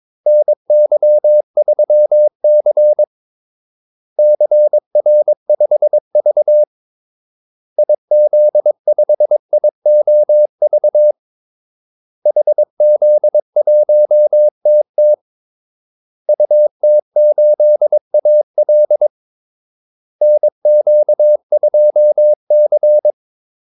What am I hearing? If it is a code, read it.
Morse code, heard as a keyed tone, about 22 wpm, 600 Hz. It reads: NY3C CR5V IZ5IOV HZ1TT UT8AL NQ2C
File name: Callsigns-22wpm